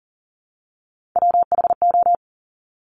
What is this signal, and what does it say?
Morse code, heard as a keyed tone, about 40 wpm, 700 Hz. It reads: WHO
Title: Top 200 Words Repeated 40wpm